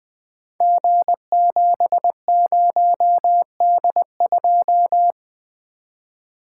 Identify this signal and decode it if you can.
Morse code, heard as a keyed tone, about 20 wpm, 700 Hz. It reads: G70D2
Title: Code Groups - 5 Letter:Numbers - 20wpm